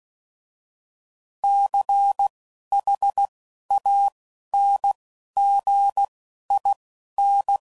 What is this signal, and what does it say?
Morse code, heard as a keyed tone, about 16 wpm, 800 Hz. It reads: C H A N G I N